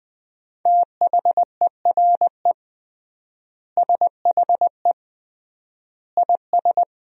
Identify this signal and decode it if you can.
Morse code, heard as a keyed tone, about 20 wpm, 700 Hz. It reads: THERE SHE IS
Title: 3x Word-Spacing - Sentences from Top 200 Words - 20wpm